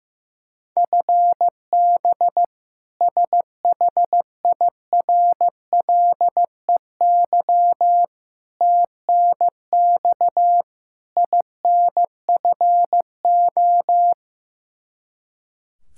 Morse code, heard as a keyed tone, about 15 wpm, 700 Hz. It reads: FB SHIRLEY TNX INFO
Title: K1USN SST QSO Practice 15wpm